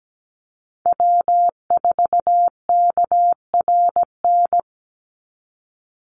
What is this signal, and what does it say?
Morse code, heard as a keyed tone, about 17 wpm, 700 Hz. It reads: W4KRN
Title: NA SOTA Chaser Callsigns 17wpm